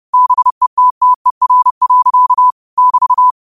Morse code, heard as a keyed tone, about 30 wpm, 1000 Hz. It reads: BETTER. <BT>